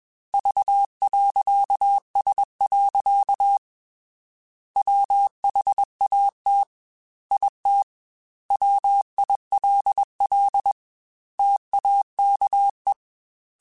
Morse code, heard as a keyed tone, about 21 wpm, 800 Hz. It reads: V.S. WHAT IT WILL TAKE
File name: Technical Subjects / Technical Subjects-mp3s / OR Gate @20